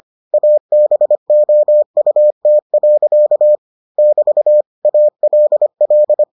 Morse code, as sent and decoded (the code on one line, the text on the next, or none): .- -... --- ..- - .-.-.- / -...- / .- .-.. .-..
ABOUT. <BT> ALL